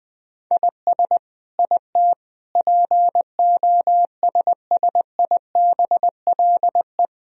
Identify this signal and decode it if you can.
Morse code, heard as a keyed tone, about 20 wpm, 700 Hz. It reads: IS IT POSSIBLE
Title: Sentences from Top 1000 Words - Review Part 3 - 20wpm